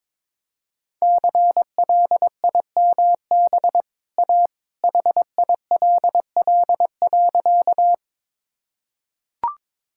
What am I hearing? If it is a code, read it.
Morse code, heard as a keyed tone, about 22 wpm, 700 Hz. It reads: CLIMB A HILL.
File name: practice / 22wpm / Easy Sentences with Hints - 22wpm